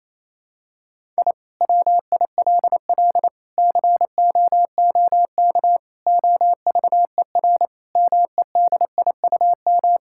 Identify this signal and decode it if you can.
Morse code, heard as a keyed tone, about 28 wpm, 700 Hz. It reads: I WILL COOK OVER MEDIUM
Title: Sentences from Top 1000 Words - Review Part 6 - 28wpm